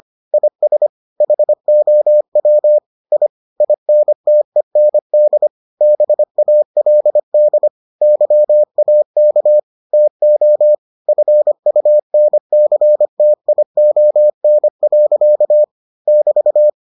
Morse code, heard as a keyed tone, about 25 wpm, 600 Hz. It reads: IS HOW I INTEND BALD YAK TO FUNCTION. <BT>